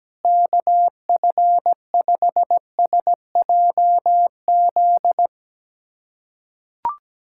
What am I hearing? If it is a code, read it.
Morse code, heard as a keyed tone, about 17 wpm, 700 Hz. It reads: KF5SJZ
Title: USA Callsign DE Callsign - 6 Characters - 17wpm